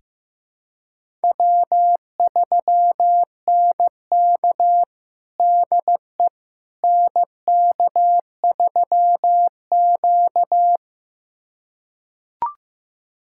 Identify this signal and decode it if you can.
Morse code, heard as a keyed tone, about 15 wpm, 700 Hz. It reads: W3NK DE NK3Q
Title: USA Callsign DE Callsign - 4 Characters - 15wpm